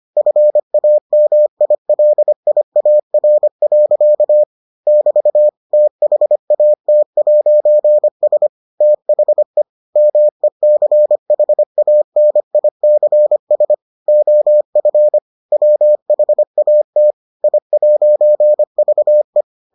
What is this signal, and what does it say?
Morse code, heard as a keyed tone, about 25 wpm, 600 Hz. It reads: FAMILIAR. <BT> THAT'S THE MECHANICS OF WHAT I'VE